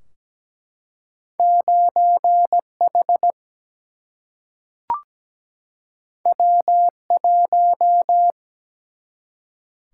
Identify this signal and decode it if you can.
Morse code, heard as a keyed tone, about 17 wpm, 700 Hz. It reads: 9 H W 1
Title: Single-Letter Single-Number 17wpm